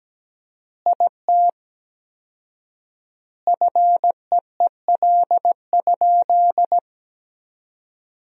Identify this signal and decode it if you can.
Morse code, heard as a keyed tone, about 17 wpm, 700 Hz. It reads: IT FEEL?
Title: 4x Word-Spacing - Sentences from Top 400 Words - 17wpm